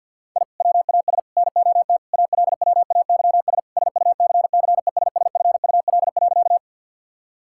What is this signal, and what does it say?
Morse code, heard as a keyed tone, about 50 wpm, 700 Hz. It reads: I WAS NOT ALWAYS SUCCESSFUL.
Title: Sentences from Top 1000 Words - Review Part 6 - 50wpm